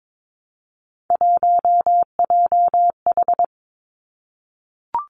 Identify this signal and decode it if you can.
Morse code, heard as a keyed tone, about 22 wpm, 700 Hz. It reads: 1JH
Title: Three Challenging Letter-Number Combinations 22wpm